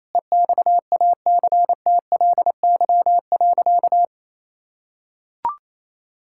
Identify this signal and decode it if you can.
Morse code, heard as a keyed tone, about 28 wpm, 690 Hz. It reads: EXACTLY.
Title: Sentences from Top 1000 Words - Review Part 6 - 28wpm